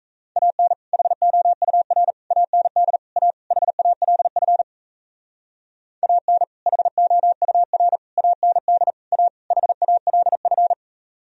Morse code, heard as a keyed tone, about 35 wpm, 700 Hz. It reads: AN HOUR AND A HALF AN HOUR AND A HALF
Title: Speed-Racing - Sets of 5 Words 28wpm